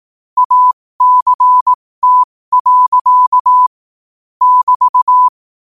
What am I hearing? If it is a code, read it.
Morse code, heard as a keyed tone, about 18 wpm, spaced as for 15 wpm, 1000 Hz. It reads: ACT. <BT>
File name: qotdfiles / 15wpm